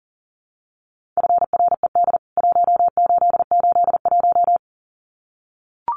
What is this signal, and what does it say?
Morse code, heard as a keyed tone, about 40 wpm, 700 Hz. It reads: FRED 1881
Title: CWT Exchange 40wpm